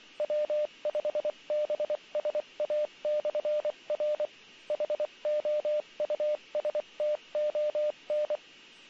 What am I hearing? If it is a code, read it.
Morse code, heard as a keyed tone, about 24 wpm, 560 Hz, with radio noise. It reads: W5BSA/R HOUSTON